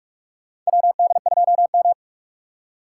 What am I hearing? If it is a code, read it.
Morse code, heard as a keyed tone, about 46 wpm, 700 Hz. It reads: WD2M
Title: USA Call Signs - 4 Characters - 6x Character Spacing - 45wpm